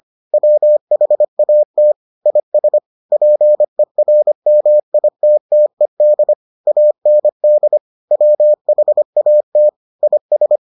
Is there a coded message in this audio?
Yes: Morse code from 0.3 to 10.6 s, heard as a keyed tone, about 25 wpm, 600 Hz. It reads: WHAT IS PERMITTED AND WHAT IS